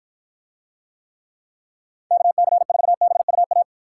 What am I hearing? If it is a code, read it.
Morse code, heard as a keyed tone, about 53 wpm, 700 Hz. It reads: KC4BUA